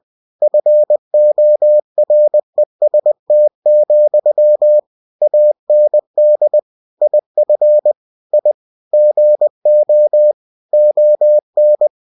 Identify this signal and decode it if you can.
Morse code, heard as a keyed tone, about 20 wpm, 600 Hz. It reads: FOREST, AND IF I GO ON
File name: cw-books / winnie-the-pooh / winnie-20wpm-page-0076